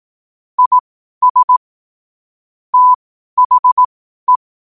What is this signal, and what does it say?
Morse code, heard as a keyed tone, about 18 wpm, spaced as for 12 wpm, 1000 Hz. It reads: IS THE
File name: qotdfiles / 12wpm